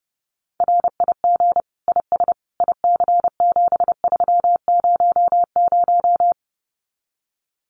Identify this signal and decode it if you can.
Morse code, heard as a keyed tone, about 28 wpm, 700 Hz. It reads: RIG IS IC7300
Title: Common QSO Phrases Encore 30wpm